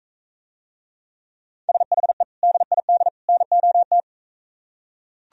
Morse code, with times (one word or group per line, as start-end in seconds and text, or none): 1.7-2.2 SHE
2.4-3.1 DID
3.3-4.0 NOT